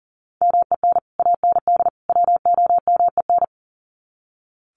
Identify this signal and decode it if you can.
Morse code, heard as a keyed tone, about 40 wpm, 700 Hz. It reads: MEN AND WOMEN
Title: Sets of 3 Words 40wpm